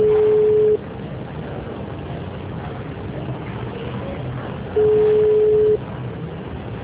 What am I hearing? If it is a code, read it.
Ringback tone (Europe, 425 Hz).